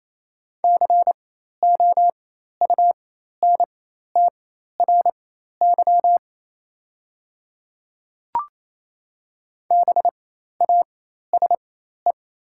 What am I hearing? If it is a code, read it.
Morse code, heard as a keyed tone, about 28 wpm, spaced as for 13 wpm, 700 Hz. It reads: COUNTRY BASE
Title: 4x Character Spacing - Top 300 Words - 28wpm